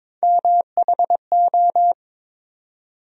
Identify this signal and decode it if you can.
Morse code, heard as a keyed tone, about 22 wpm, 700 Hz. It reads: MHO
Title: USA Call Sign Postfixes - 3 Letters 22wpm